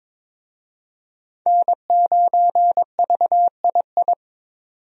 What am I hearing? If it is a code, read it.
Morse code, heard as a keyed tone, about 21 wpm, 700 Hz. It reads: N9VII
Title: USA Call Signs - 5 Characters - 22wpm